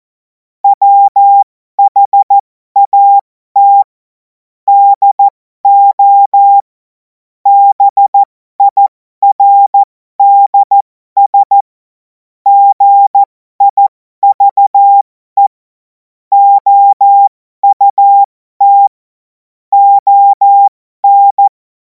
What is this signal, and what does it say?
Morse code, heard as a keyed tone, about 14 wpm, spaced as for 12 wpm, 800 Hz. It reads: WHAT DO BIRDS GIVE OUT ON
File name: Morsekurs / Witze